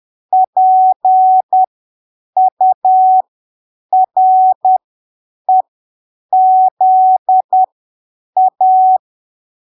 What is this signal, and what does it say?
Morse code, heard as a keyed tone, about 10 wpm, spaced as for 7 wpm, 750 Hz. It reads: PUREZA
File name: arquivos / texto / texto 11 10ppm